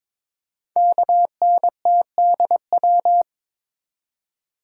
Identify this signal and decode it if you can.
Morse code, heard as a keyed tone, about 22 wpm, 700 Hz. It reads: KNTDW
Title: Code Groups - 5 Letters - 22wpm